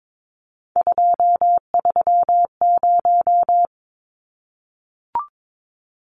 Morse code, heard as a keyed tone, about 22 wpm, 700 Hz. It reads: 230
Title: Numbers - 3-Digit 22wpm